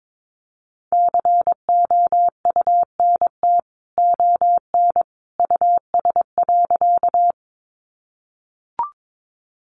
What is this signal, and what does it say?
Morse code, heard as a keyed tone, about 22 wpm, 700 Hz. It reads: COUNT ON US.
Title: Sentences from Top 1500 Words 22wpm